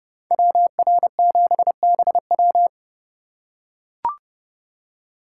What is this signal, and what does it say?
Morse code, heard as a keyed tone, about 29 wpm, 700 Hz. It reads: WR7BW